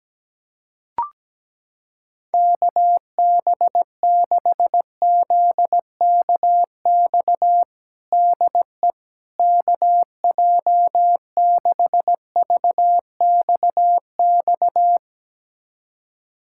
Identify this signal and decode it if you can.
Morse code, heard as a keyed tone, about 17 wpm, 700 Hz. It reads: KB6ZKX DE KJ6VXX
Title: USA Callsign DE Callsign - 6 Characters - 17wpm